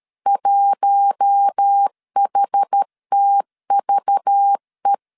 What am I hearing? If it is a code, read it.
Morse code, heard as a keyed tone, about 13 wpm, 800 Hz. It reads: 1HTVE
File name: LEZIONI / 6 PKX0 / 6 PKX0 12wpm 2231